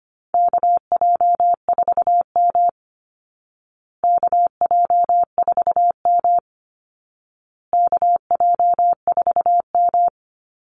Morse code, heard as a keyed tone, about 25 wpm, 700 Hz. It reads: KJ4M KJ4M KJ4M
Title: Call Signs Repeated 2x - 25wpm